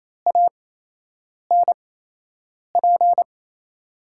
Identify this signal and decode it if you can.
Morse code, heard as a keyed tone, about 27 wpm, spaced as for 8 wpm, 700 Hz. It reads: ANP
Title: USA Call Signs - 6 Characters - 8x Character Spacing - 28wpm